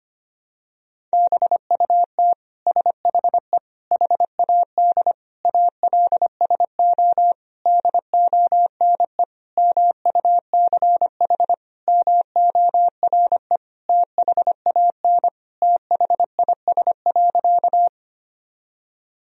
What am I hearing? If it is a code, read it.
Morse code, heard as a keyed tone, about 25 wpm, 700 Hz. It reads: BUT SHE HAD ALSO DONE MUCH MORE THAN THIS.
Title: Sentences from Top 1000 Words - Review Part 1 - 25wpm